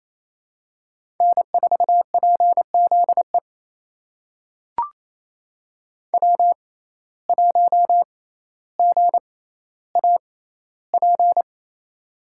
Morse code, heard as a keyed tone, about 28 wpm, spaced as for 14 wpm, 700 Hz. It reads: N4PZE W1GAP
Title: USA Call Signs - 5 Characters - 6x Character Spacing - 28wpm